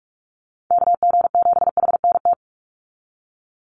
Morse code, heard as a keyed tone, about 44 wpm, 700 Hz. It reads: KG7HNT
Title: USA Call Signs 45wpm